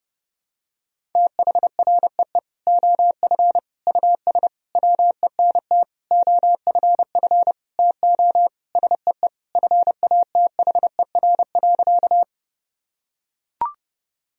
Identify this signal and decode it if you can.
Morse code, heard as a keyed tone, about 32 wpm, 700 Hz. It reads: THREE OF US WENT OFF TO SEE FATHER.
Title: Sentences from Top 1000 Words - Review Part 1 - 30wpm